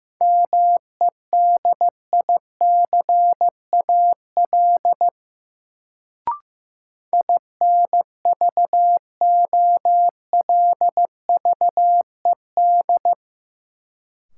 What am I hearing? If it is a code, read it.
Morse code, heard as a keyed tone, about 15 wpm, 700 Hz. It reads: MEDICAL INVOLVED
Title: Sentences from Top 800 Words - New Word Drill - 15wpm